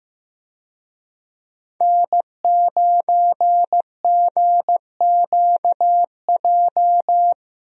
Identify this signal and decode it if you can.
Morse code, heard as a keyed tone, about 15 wpm, 700 Hz. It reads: N9GQJ
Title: USA Call Signs - 5 Characters - Repeated 3x - 15wpm